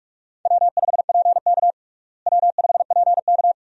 Morse code, heard as a keyed tone, about 43 wpm, 700 Hz. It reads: W5PK W5PK
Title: USA Call Signs - 4 Characters - Repeated 2x - 45wpm